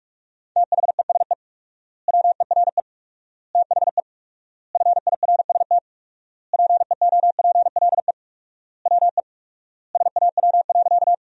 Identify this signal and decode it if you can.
Morse code, heard as a keyed tone, about 41 wpm, 700 Hz. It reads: THESE WERE THE FIRST PEOPLE WE SAW.
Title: 4x Word-Spacing - Sentences from Top 300 Words - 45wpm